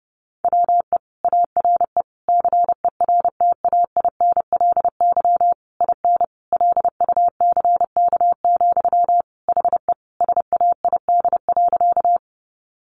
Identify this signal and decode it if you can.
Morse code, heard as a keyed tone, about 30 wpm, 700 Hz. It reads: WE ARE CERTAINLY IN LUCK, HE SAID.
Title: Sentences from Top 1500 Words 30wpm